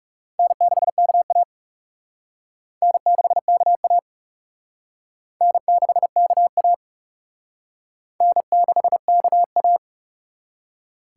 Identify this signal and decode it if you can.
Morse code, heard as a keyed tone, about 37 wpm, 700 Hz. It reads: N6KA N6KA N6KA N6KA
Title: Speed Racing - USA 1x2 Call Signs 30wpm